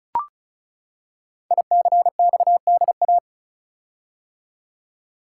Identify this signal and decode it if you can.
Morse code, heard as a keyed tone, about 36 wpm, 700 Hz. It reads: ICXDA
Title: Code Groups - 5 Letter:Numbers - 35wpm